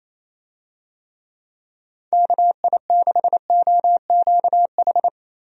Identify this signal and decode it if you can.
Morse code, heard as a keyed tone, about 28 wpm, 700 Hz. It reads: KI6OQH
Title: USA Call Signs - 6 Characters - 28wpm